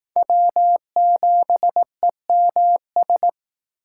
Morse code, heard as a keyed tone, about 19 wpm, 700 Hz. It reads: W7EMS
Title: Speed Racing - USA 1x3 Call Signs 15wpm